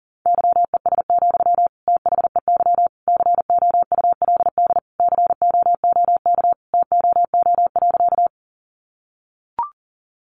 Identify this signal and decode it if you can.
Morse code, heard as a keyed tone, about 40 wpm, 700 Hz. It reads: YES, THEY COULD COOK TOO.
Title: Sentences from Top 700 Words 40wpm